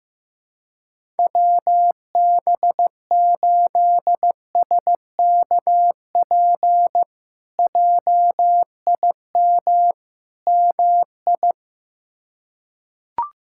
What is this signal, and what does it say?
Morse code, heard as a keyed tone, about 15 wpm, 700 Hz. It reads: WB8SKP JIM MI
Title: K1USN SST QSO Practice - 15wpm